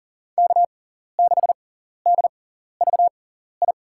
Morse code, heard as a keyed tone, about 37 wpm, spaced as for 14 wpm, 690 Hz. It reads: K6DVI
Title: USA Call Signs - 6x Character Spacing - 40wpm